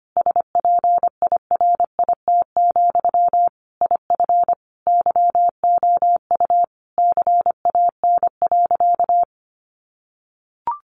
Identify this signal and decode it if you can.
Morse code, heard as a keyed tone, about 25 wpm, 700 Hz. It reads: SPIRIT, IF YOU CAN.
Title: Sentences from Top 1400 Words 25wpm